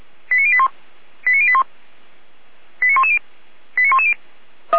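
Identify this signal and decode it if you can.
Five-tone selective call (ZVEI), heard as quick runs of five tones, tones of about 70 ms. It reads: 89081 89081 89110 89110
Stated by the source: ZVEI five-tone selective call, code 89508, received via SDR.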